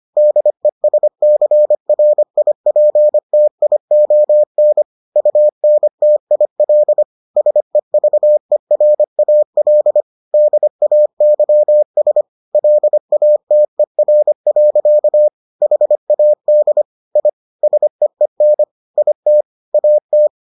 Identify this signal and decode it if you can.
Morse code, heard as a keyed tone, about 25 wpm, 600 Hz. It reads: DESCRIPTION UNTIL SEVERAL DAYS LATER. HAD I SEEN IT AT